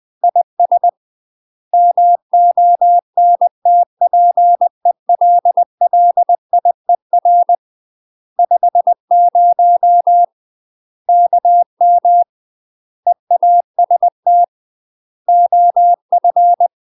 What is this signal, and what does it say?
Morse code, heard as a keyed tone, about 20 wpm, 700 Hz. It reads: IS MONTPELLIER 50 KM EAST OF
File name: UK MASTER QSOs TRAINING/20WPM